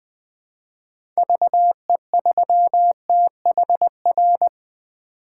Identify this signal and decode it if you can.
Morse code, heard as a keyed tone, about 20 wpm, 700 Hz. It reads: VE3THR
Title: 2023 POTA Activator Call Signs - 20wpm